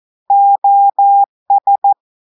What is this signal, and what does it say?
Morse code, heard as a keyed tone, about 14 wpm, 800 Hz. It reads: OS
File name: audios / portugais / portugais texte en clair 1 vitesse 14 mots minutes